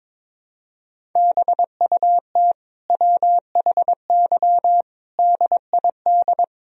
Morse code, heard as a keyed tone, about 22 wpm, 700 Hz. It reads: BUT WHY DID
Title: Sentences from Top 1000 Words - Review Part 1 - 22wpm